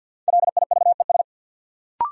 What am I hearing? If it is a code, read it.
Morse code, heard as a keyed tone, about 51 wpm, 700 Hz. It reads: LIVES
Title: Sets of 5 Words Encore - New Word Drill - 50wpm